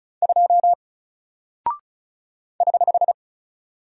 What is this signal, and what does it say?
Morse code, heard as a keyed tone, about 36 wpm, 700 Hz. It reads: 2 <HH>
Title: Single Character (Letter:Number:Punctuation:Prosign) Alt 35wpm